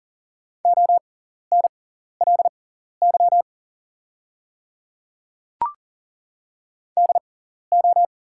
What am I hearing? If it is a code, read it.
Morse code, heard as a keyed tone, about 40 wpm, spaced as for 14 wpm, 690 Hz. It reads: ONLY DO
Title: 6x Character Spacing - Top 200 Words - 40wpm